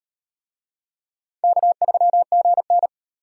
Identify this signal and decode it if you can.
Morse code, heard as a keyed tone, about 38 wpm, 700 Hz. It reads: K3GN